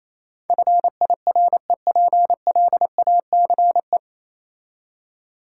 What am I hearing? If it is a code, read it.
Morse code, heard as a keyed tone, about 27 wpm, 700 Hz. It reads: FIREPLACE
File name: Easy Words - Speed-Racing with Hints - 22wpm